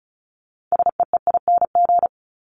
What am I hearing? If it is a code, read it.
Morse code, heard as a keyed tone, about 34 wpm, 700 Hz. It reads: SEEING
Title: ING Suffix - Top 100 Words with Base Words 35wpm